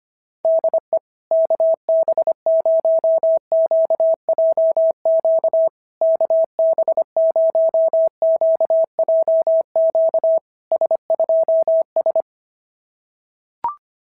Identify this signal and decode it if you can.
Morse code, heard as a keyed tone, about 25 wpm, 660 Hz. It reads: DE KB0QJQ KB0QJQ S2S
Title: SOTA QSO Practice 25wpm